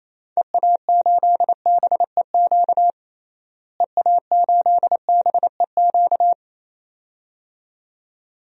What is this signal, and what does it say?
Morse code, heard as a keyed tone, about 28 wpm, 700 Hz. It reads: EA8BEQ EA8BEQ
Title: DXCC-100 Call Signs - 6 Characters - Repeated 2x - 28wpm